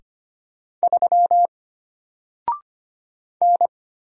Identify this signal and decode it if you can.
Morse code, heard as a keyed tone, about 25 wpm, 700 Hz. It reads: 3 N
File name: Single-Letter-Number-25wpm